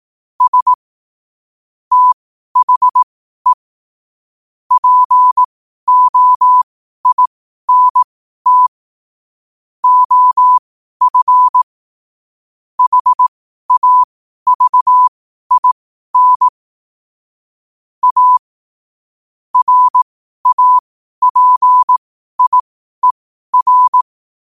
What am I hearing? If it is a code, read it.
Morse code, heard as a keyed tone, about 18 wpm, spaced as for 12 wpm, 1000 Hz. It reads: S THE POINT OF HAVIN A RAPIER
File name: qotdfiles / 12wpm